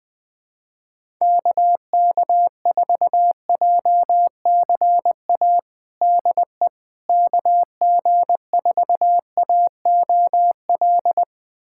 Morse code, heard as a keyed tone, about 20 wpm, 700 Hz. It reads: KK4JCA DE KG4AOL
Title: USA Callsign DE Callsign - 6 Characters - 20wpm